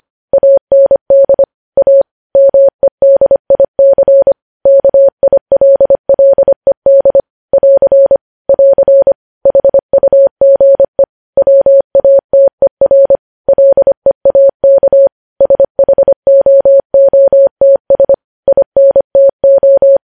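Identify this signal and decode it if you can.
Morse code, heard as a keyed tone, about 25 wpm, 550 Hz. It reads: AND A MEDIC KILLED <AR> <AR> HUGE WATER LEAK SHOOTS INTO
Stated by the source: Morse code news headlines sent at 25 wpm.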